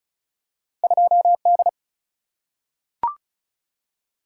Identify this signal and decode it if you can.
Morse code, heard as a keyed tone, about 34 wpm, 700 Hz. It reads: 2D
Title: Single-Letter Single-Number 35wpm